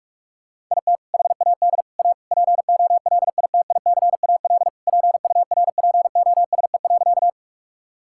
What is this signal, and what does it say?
Morse code, heard as a keyed tone, about 46 wpm, 700 Hz. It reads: IT HAD A POLITICAL PURPOSE.
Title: Sentences from Top 1000 Words - Review Part 5 - 45wpm